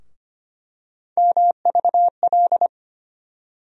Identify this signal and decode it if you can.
Morse code, heard as a keyed tone, about 25 wpm, 700 Hz. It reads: MVL
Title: USA Call Sign Postfixes - 3 Letters 25wpm